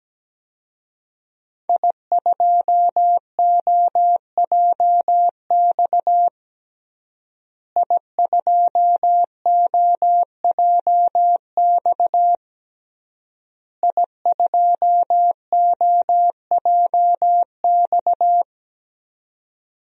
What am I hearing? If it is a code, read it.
Morse code, heard as a keyed tone, about 17 wpm, 700 Hz. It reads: I2OJX I2OJX I2OJX
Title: DXCC-100 Call Signs - 5 Characters - Repeated 3x - 17wpm